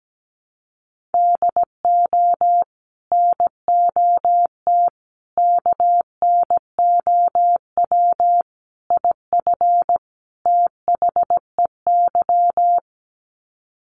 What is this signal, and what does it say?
Morse code, heard as a keyed tone, about 17 wpm, 700 Hz. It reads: DO NOT KNOW IF THEY
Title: Sets of 5 Words Encore 17wpm